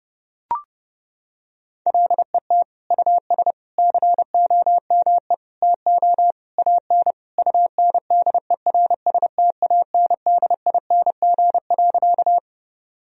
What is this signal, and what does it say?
Morse code, heard as a keyed tone, about 30 wpm, 700 Hz. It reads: LET US COME TO AN UNDERSTANDING.
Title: Sentences from Top 1000 Words - Review Part 3 - 30wpm